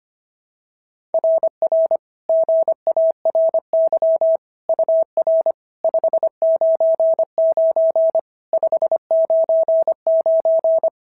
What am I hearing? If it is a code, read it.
Morse code, heard as a keyed tone, about 24 wpm, 650 Hz. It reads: RR GARY UR 599 599
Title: SOTA QSO Practice 25wpm